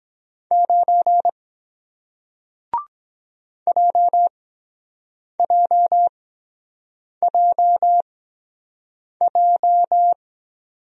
Morse code, heard as a keyed tone, about 23 wpm, spaced as for 7 wpm, 700 Hz. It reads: 9 JJJJ